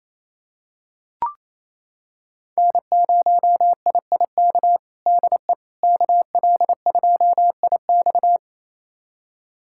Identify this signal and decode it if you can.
Morse code, heard as a keyed tone, about 27 wpm, 700 Hz. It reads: N0IIK DE KL2IX
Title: USA Callsign DE Callsign - 5 Characters - 28wpm